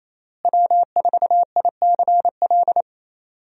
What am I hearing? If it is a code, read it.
Morse code, heard as a keyed tone, about 28 wpm, 700 Hz. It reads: W4ICL
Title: USA Call Signs - 5 Characters - Repeated 3x - 28wpm